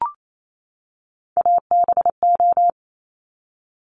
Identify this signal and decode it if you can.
Morse code, heard as a keyed tone, about 28 wpm, 700 Hz. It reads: ABO
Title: Top Three Letter Combinations in Words 28wpm